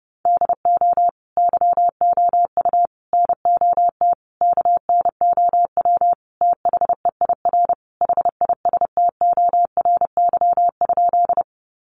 Morse code, heard as a keyed tone, about 28 wpm, 700 Hz. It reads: DO YOU NOT KNOW THEIR HISTORY?
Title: Sentences from Top 600 Words 30wpm